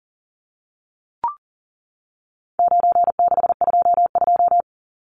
Morse code, heard as a keyed tone, about 40 wpm, 700 Hz. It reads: 9622
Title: Numbers - 4-Digit 40wpm